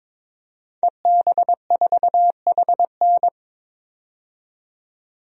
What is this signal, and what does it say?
Morse code, heard as a keyed tone, about 22 wpm, 700 Hz. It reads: EB4HN
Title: DXCC-100 Call Signs - 5 Characters - 22wpm